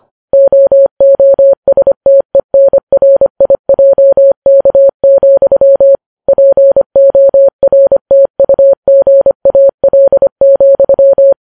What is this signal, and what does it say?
Morse code, heard as a keyed tone, about 25 wpm, 550 Hz. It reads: OOSTENRIJK, PORTUGAL,